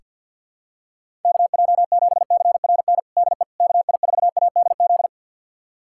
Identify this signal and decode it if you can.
Morse code, heard as a keyed tone, about 48 wpm, 700 Hz. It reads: KJ7CRN DE KI4ADZ
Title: USA Callsign DE Callsign - 6 Characters - 50wpm